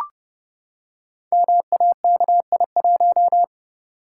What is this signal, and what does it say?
Morse code, heard as a keyed tone, about 30 wpm, 700 Hz. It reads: MAKI1
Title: Code Groups - 5 Letter:Numbers - 30wpm